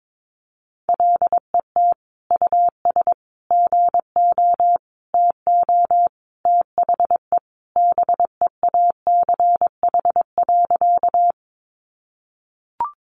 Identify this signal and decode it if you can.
Morse code, heard as a keyed tone, about 22 wpm, 700 Hz. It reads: LET US GO TO THE BEACH.
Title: Sentences from Top 1500 Words 22wpm